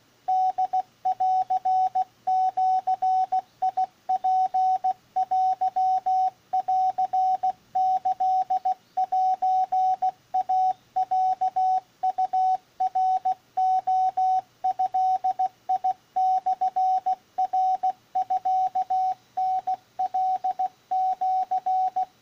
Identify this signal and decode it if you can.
Morse code (Japanese Wabun code), heard as a keyed tone, about 15 wpm, 750 Hz, with radio noise. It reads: ホンジツテンキセイロウナレドモナミタカシ